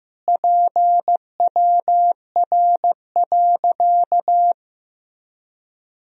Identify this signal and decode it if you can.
Morse code, heard as a keyed tone, about 15 wpm, 700 Hz. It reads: PWR.
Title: Sentences from Top 500 Words and Abbreviations 15wpm